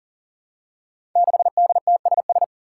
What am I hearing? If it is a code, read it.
Morse code, heard as a keyed tone, about 40 wpm, 700 Hz. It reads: 6DTSS
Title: Five Challenging Letter-Number Combinations 40wpm